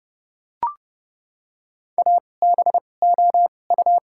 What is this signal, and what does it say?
Morse code, heard as a keyed tone, about 30 wpm, spaced as for 22 wpm, 700 Hz. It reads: ABOU
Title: 2x Character Spacing - Top 100 Words - 30wpm